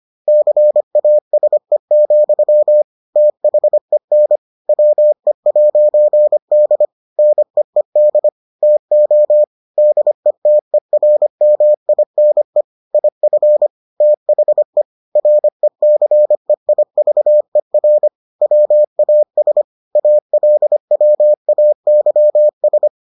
Morse code, heard as a keyed tone, about 25 wpm, 600 Hz. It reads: CASE, THEN WE'D NEED TO DETERMINE IF THE RECEIVER WAS ALWAYS